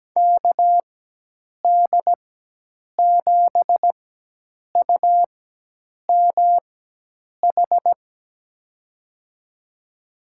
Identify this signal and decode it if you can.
Morse code, heard as a keyed tone, about 17 wpm, spaced as for 8 wpm, 700 Hz. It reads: KD7UMH
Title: USA Call Signs - 6 Characters - 4x Character Spacing - 17wpm